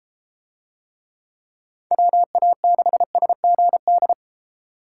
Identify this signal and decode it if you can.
Morse code, heard as a keyed tone, about 32 wpm, 700 Hz. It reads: WA6SGD